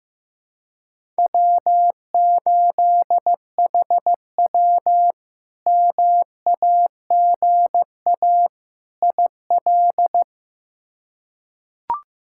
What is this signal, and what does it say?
Morse code, heard as a keyed tone, about 15 wpm, 700 Hz. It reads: W8HW MAGA IL
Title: K1USN SST QSO Practice - 15wpm